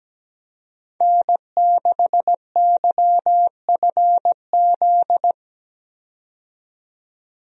Morse code, heard as a keyed tone, about 17 wpm, 700 Hz. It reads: N6YFZ
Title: USA Call Signs - 5 Characters - 17wpm